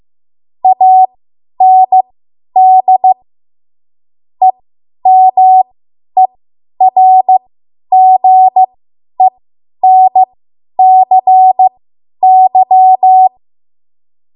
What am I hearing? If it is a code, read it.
Morse code, heard as a keyed tone, about 15 wpm, spaced as for 10 wpm, 750 Hz. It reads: AND EMERGENCY
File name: Morse / 10WPM